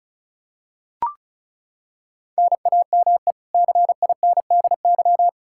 Morse code, heard as a keyed tone, about 34 wpm, 700 Hz. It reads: NAME CINDY
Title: K1USN SST QSO Practice 35wpm